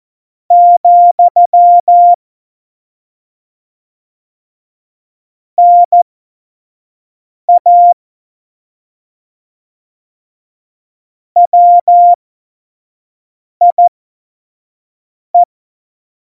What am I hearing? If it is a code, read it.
Morse code, heard as a keyed tone, about 14 wpm, spaced as for 5 wpm, 700 Hz. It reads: , NA WIE
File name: Morsekurs / Witze / Witz 02. Juli 2019, 25 F70